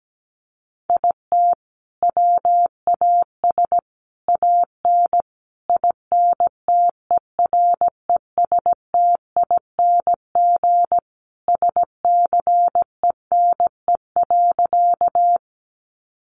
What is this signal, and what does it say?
Morse code, heard as a keyed tone, about 17 wpm, 700 Hz. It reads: IT WAS AN INTERESTING SCENE.
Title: Sentences from Top 1100 Words 17wpm